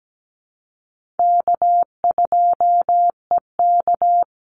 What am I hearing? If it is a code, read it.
Morse code, heard as a keyed tone, about 17 wpm, 700 Hz. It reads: K2EK
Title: Easy Call Signs - 17wpm